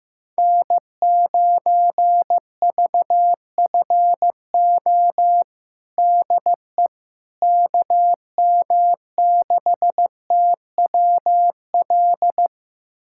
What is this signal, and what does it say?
Morse code, heard as a keyed tone, about 15 wpm, 700 Hz. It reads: N9VFO DE KM6TWL
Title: USA Callsign DE Callsign 15wpm